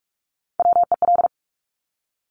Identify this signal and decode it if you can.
Morse code, heard as a keyed tone, about 46 wpm, 700 Hz. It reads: WEL
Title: Top Three Letter Combinations in Words 45wpm